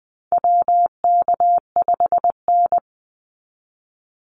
Morse code, heard as a keyed tone, about 20 wpm, 700 Hz. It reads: WK5N
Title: Call Signs - 20wpm